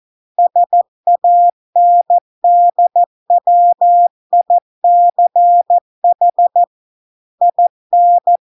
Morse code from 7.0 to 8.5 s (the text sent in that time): IN